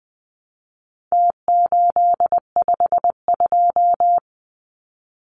Morse code, heard as a keyed tone, about 20 wpm, 700 Hz. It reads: T852
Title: Numbers - Contest Style - 4-Digit 20wpm